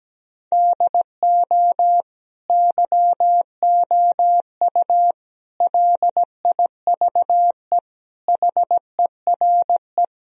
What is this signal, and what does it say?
Morse code, heard as a keyed tone, about 17 wpm, 700 Hz. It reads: DO YOU LIVE HERE
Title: Sentences from Top 1000 Words - Review Part 1 - 17wpm